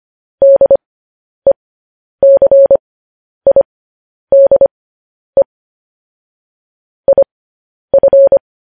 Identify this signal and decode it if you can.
Morse code, heard as a keyed tone, about 25 wpm, spaced as for 10 wpm, 550 Hz. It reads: DECIDE IF